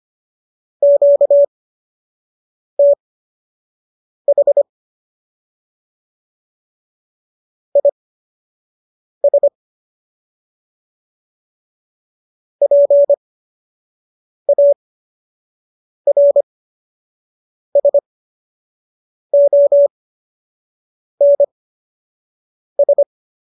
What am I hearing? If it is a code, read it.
Morse code, heard as a keyed tone, about 25 wpm, spaced as for 6 wpm, 570 Hz. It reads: QTH IS PARSONS
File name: qso003 06